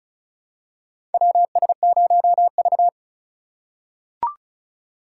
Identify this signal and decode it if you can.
Morse code, heard as a keyed tone, about 36 wpm, 700 Hz. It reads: WS0V
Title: USA Call Signs - 3x Character Spacing - 35wpm